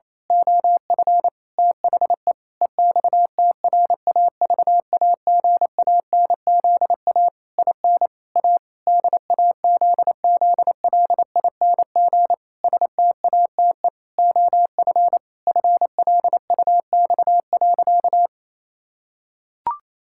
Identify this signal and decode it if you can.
Morse code, heard as a keyed tone, about 28 wpm, 700 Hz. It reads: OF THE EXTRAVAGANZA IN A DAZZLING STATE OF FLUX.
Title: Pangram Challenge v2 - 28wpm